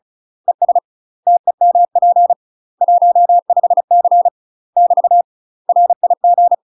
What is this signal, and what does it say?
Morse code, heard as a keyed tone, about 34 wpm, 700 Hz. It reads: ES TEMP 15C <BT> RIG
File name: UK MASTER QSOs TRAINING / 35WPM / UK MASTER QSOs 0001